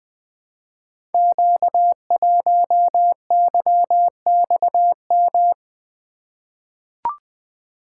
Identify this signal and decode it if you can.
Morse code, heard as a keyed tone, about 20 wpm, 700 Hz. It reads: Q1YXM
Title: Code Groups - 5 Letter:Numbers - 20wpm